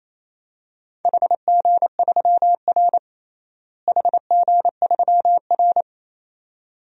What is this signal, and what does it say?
Morse code, heard as a keyed tone, about 28 wpm, 700 Hz. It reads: HG3R HG3R
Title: DXCC-100 Call Signs - 4 Characters - Repeated 2x - 28wpm